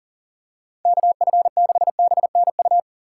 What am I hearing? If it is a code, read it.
Morse code, heard as a keyed tone, about 37 wpm, 700 Hz. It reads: KF6BNU
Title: USA Call Signs - 6 Characters - 4x Character Spacing - 40wpm